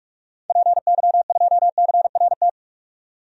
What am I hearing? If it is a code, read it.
Morse code, heard as a keyed tone, about 46 wpm, 700 Hz. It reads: PY2CRT
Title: DXCC-100 Call Signs - 6 Characters - 45wpm